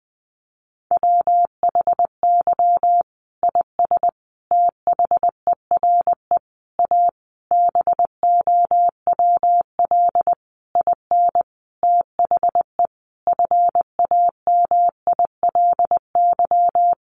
Morse code, heard as a keyed tone, about 20 wpm, 700 Hz. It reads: WHY IS THERE A BOWL IN THE FAMILY
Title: Sentences from Top 1200 Words 20wpm